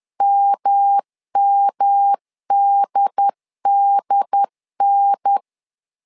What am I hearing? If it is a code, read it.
Morse code, heard as a keyed tone, about 10 wpm, 800 Hz. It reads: MMDDN
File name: LEZIONI / 3 MWRO / 3 MW 10wpm 1959